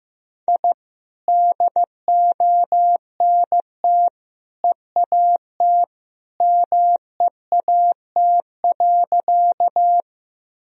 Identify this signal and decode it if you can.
Morse code, heard as a keyed tone, about 15 wpm, 700 Hz. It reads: I DONT EAT MEAT.
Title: Sentences from Top 1000 Words - Review Part 3 - 15wpm